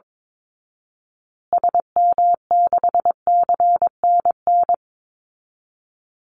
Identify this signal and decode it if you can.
Morse code, heard as a keyed tone, about 22 wpm, 700 Hz. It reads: SM6CNN
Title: Call Signs Repeated 2x - 22wpm